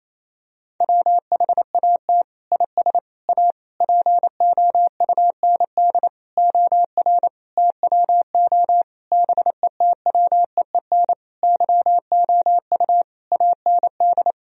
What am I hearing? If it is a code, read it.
Morse code, heard as a keyed tone, about 27 wpm, 700 Hz. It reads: WHAT IS A POUND OR TWO BETWEEN YOU AND
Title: Sentences from Top 1000 Words - Review Part 2 - 28wpm